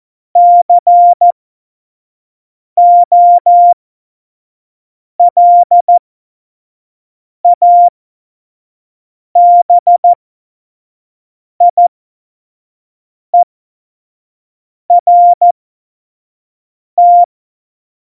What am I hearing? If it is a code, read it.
Morse code, heard as a keyed tone, about 14 wpm, spaced as for 5 wpm, 700 Hz. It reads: COLABIERT
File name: Morsekurs / Witze / Witz 01. Oktober 2019, 25 F70